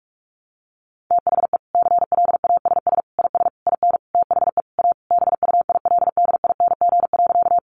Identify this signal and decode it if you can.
Morse code, heard as a keyed tone, about 44 wpm, 700 Hz. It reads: THE CLASS IS IN THE A BUILDING.
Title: Additional Easy Phrases 45wpm